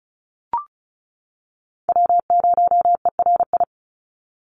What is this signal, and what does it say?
Morse code, heard as a keyed tone, about 34 wpm, 700 Hz. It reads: W0ERI
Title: NA SOTA Chaser Callsigns 35wpm